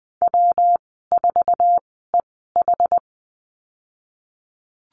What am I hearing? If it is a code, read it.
Morse code, heard as a keyed tone, about 20 wpm, spaced as for 15 wpm, 700 Hz. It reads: W4EH
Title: 193 Lesson 31g - Letter v - Call Signs 20wpm